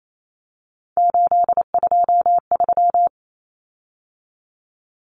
Morse code, heard as a keyed tone, about 28 wpm, 700 Hz. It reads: 823
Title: Common Cut Numbers - 3-Digit 28wpm